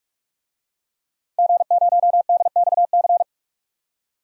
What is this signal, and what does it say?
Morse code, heard as a keyed tone, about 41 wpm, 700 Hz. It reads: G0DXC
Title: DXCC-100 Call Signs - 5 Characters - 45wpm